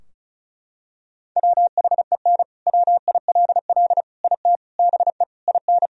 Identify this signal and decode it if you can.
Morse code, heard as a keyed tone, about 36 wpm, 700 Hz. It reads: WHEN WILL IT BE IN
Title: Sentences from Top 1000 Words - Review Part 4 - 35wpm